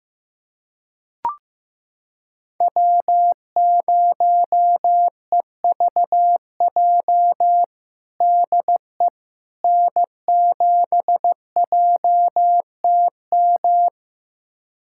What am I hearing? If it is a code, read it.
Morse code, heard as a keyed tone, about 15 wpm, 700 Hz. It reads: W0EVJ DE N7JTM
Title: USA Callsign DE Callsign - 5 Characters - 15wpm